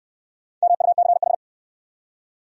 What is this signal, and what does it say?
Morse code, heard as a keyed tone, about 65 wpm, 700 Hz. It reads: MUCH